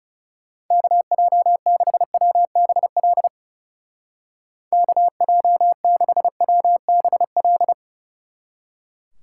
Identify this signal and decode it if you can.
Morse code, heard as a keyed tone, about 37 wpm, 700 Hz. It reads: KJ6WBL KJ6WBL
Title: Speed Racing - USA Call Signs 30wpm